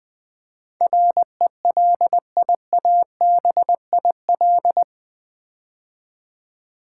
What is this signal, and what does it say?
Morse code, heard as a keyed tone, about 20 wpm, 700 Hz. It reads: RELIABIL
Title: Build A Word - 4 Letters At A Time - 11-Letter Words - 20wpm